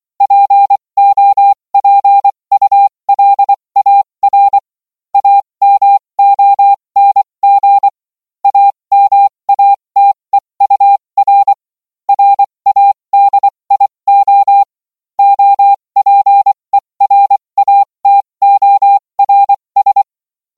Morse code, heard as a keyed tone, about 24 wpm, spaced as for 20 wpm, 800 Hz. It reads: POPULAR AMONG AMATEUR RADIO OPERATORS